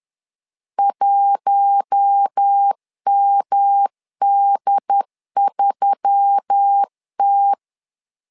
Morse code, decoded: 1MD3T

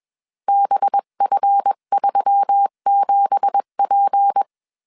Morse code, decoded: BF37P

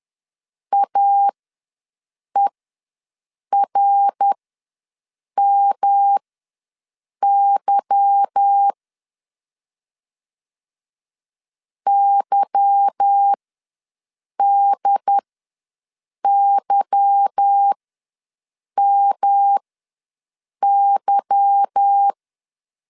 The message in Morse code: AERMY YDYMY

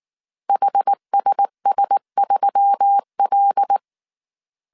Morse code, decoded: HSS3L

